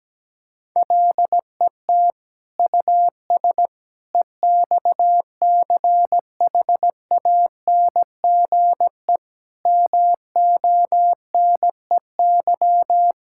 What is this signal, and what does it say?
Morse code, heard as a keyed tone, about 17 wpm, 700 Hz. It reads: LET US EXCHANGE MONEY
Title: Sentences from Top 1000 Words - Review Part 6 - 17wpm